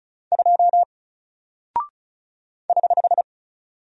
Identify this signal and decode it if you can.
Morse code, heard as a keyed tone, about 36 wpm, 700 Hz. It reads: 2 <HH>
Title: Single Character (Letter:Number:Punctuation:Prosign) Alt 35wpm